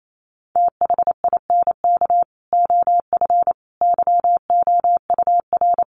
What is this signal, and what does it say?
Morse code, heard as a keyed tone, about 28 wpm, 700 Hz. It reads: THINK OF YOUR
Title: Sentences from Top 1200 Words 28wpm